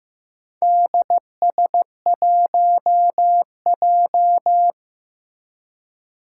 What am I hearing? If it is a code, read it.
Morse code, heard as a keyed tone, about 15 wpm, 700 Hz. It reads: DS1J
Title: Four Challenging Letter-Number Combinations 15wpm